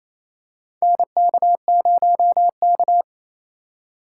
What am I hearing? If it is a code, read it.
Morse code, heard as a keyed tone, about 27 wpm, 700 Hz. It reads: NK0K
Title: USA Call Signs - 4 Characters - 4x Character Spacing - 28wpm